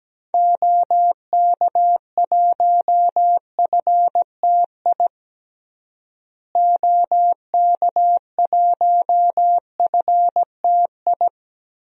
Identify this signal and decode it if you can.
Morse code, heard as a keyed tone, about 17 wpm, 700 Hz. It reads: OK1FTI OK1FTI
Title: DXCC-100 Call Signs - 6 Characters - Repeated 2x - 17wpm